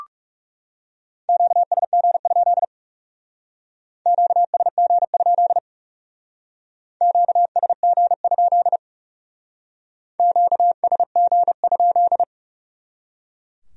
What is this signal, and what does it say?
Morse code, heard as a keyed tone, about 36 wpm, 700 Hz. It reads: QSG? QSG? QSG? QSG?
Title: Speed-Racing Q-Codes - 30wpm